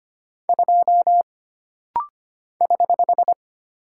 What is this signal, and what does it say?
Morse code, heard as a keyed tone, about 25 wpm, 700 Hz. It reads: 2 <HH>
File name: Single Character (Letter:Number:Punctuation:Prosign) - Rapid-Fire - 25wpm